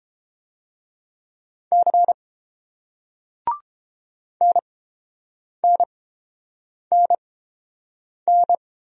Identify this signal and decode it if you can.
Morse code, heard as a keyed tone, about 30 wpm, spaced as for 7 wpm, 700 Hz. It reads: C NNNN